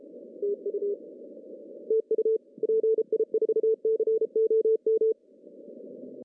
Morse code, heard as a keyed tone, about 34 wpm, 410 Hz. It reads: TU TU PI4COM
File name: CW / cqww-cw-k5tr